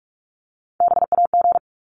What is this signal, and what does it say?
Morse code, heard as a keyed tone, about 44 wpm, 700 Hz. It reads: BAG
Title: Top 1501-1750 Words 45wpm